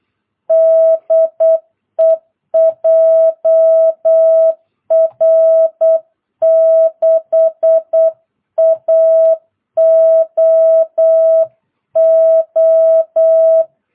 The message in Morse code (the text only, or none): DE JR6AOO